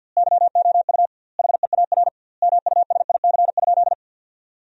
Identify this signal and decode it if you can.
Morse code, heard as a keyed tone, about 50 wpm, 700 Hz. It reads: YOU HEAR MUSIC?
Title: 2x Word-Spacing - Sentences from Top 300 Words - 50wpm